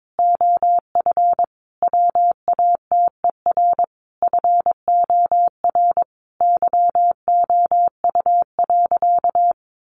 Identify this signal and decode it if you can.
Morse code, heard as a keyed tone, about 22 wpm, 700 Hz. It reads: OF WATER FOR YOU.
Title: Sentences from Top 1300 Words 22wpm